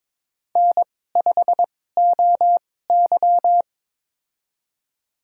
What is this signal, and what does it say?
Morse code, heard as a keyed tone, about 22 wpm, spaced as for 16 wpm, 700 Hz. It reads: N5OY
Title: USA Call Signs - 4 Characters - 2x Character Spacing - 22wpm